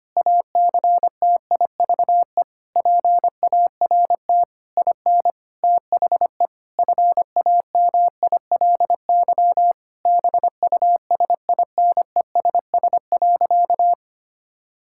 Morse code, heard as a keyed tone, about 25 wpm, 700 Hz. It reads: ACTIVE PART IN THE FAMILY BUSINESS.
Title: Sentences from Top 1000 Words - Review Part 6 - 25wpm